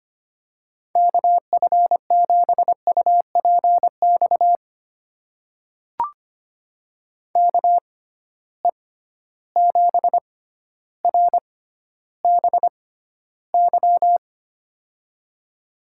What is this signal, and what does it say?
Morse code, heard as a keyed tone, about 25 wpm, spaced as for 13 wpm, 700 Hz. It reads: KF7UPX KE7RBY
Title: USA Call Signs - 6 Characters - 6x Character Spacing - 25wpm